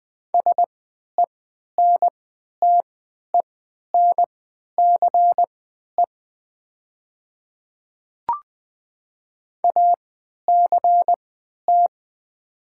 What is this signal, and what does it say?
Morse code, heard as a keyed tone, about 20 wpm, spaced as for 11 wpm, 700 Hz. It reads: SENTENCE ACT
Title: 3x Character Spacing - Top 300 Words - 20wpm